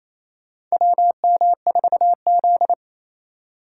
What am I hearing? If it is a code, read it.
Morse code, heard as a keyed tone, about 27 wpm, 700 Hz. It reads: WM4Z
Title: USA Call Signs - 4 Characters - 4x Character Spacing - 28wpm